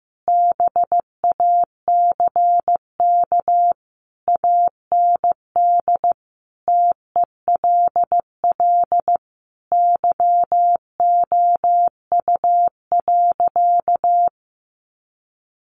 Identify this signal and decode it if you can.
Morse code, heard as a keyed tone, about 15 wpm, 700 Hz. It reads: BACK AND TELL YOU.
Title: Sentences from Top 1100 Words 15wpm